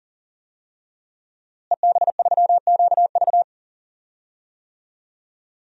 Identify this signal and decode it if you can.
Morse code, heard as a keyed tone, about 38 wpm, 700 Hz. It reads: EB3QV
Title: DXCC-100 Call Signs - 5 Characters - Repeated 3x - 40wpm